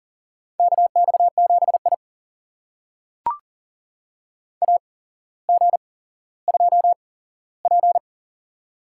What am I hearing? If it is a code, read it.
Morse code, heard as a keyed tone, about 39 wpm, spaced as for 17 wpm, 700 Hz. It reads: KX7I AG2P
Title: USA Call Signs - 4 Characters - 8x Character Spacing - 40wpm